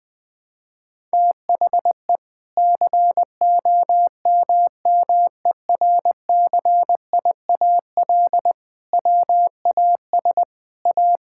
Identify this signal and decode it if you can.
Morse code, heard as a keyed tone, about 21 wpm, 700 Hz. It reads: THE COMMERCIAL WAS A
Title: Sentences from Top 1000 Words - Review Part 5 - 20wpm